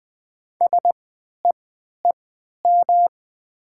Morse code, heard as a keyed tone, about 20 wpm, spaced as for 11 wpm, 690 Hz. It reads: SEEM
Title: 3x Character Spacing - Top 300 Words - 20wpm